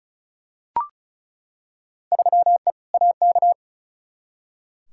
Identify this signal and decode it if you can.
Morse code, heard as a keyed tone, about 36 wpm, 700 Hz. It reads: 3E AK
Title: Field Day Exchanges Encore - 35wpm